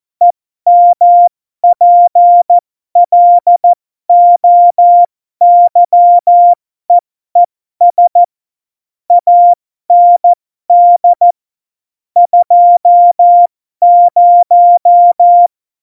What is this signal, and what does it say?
Morse code, heard as a keyed tone, about 14 wpm, spaced as for 12 wpm, 700 Hz. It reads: EMPLOYEES AND 20
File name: Morsekurs / Witze